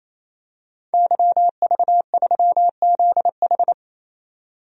Morse code, heard as a keyed tone, about 28 wpm, 700 Hz. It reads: YV3ZH
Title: DXCC-100 Call Signs - 5 Characters - 28wpm